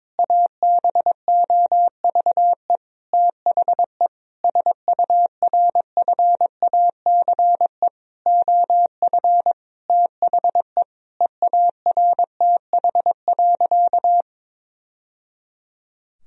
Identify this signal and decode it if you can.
Morse code, heard as a keyed tone, about 22 wpm, 700 Hz. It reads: ABOVE THE SURFACE OF THE EARTH.
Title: Sentences from Top 1000 Words - Review Part 3 - 22wpm